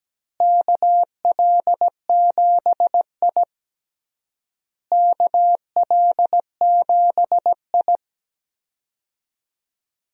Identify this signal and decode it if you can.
Morse code, heard as a keyed tone, about 17 wpm, 700 Hz. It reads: KL7I KL7I
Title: DXCC-100 Call Signs - 4 Characters - Repeated 3x - 17wpm